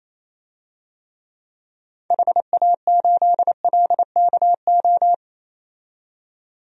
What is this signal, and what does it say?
Morse code, heard as a keyed tone, about 28 wpm, 700 Hz. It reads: HA8LKO
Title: DXCC-100 Call Signs - 6 Characters - 28wpm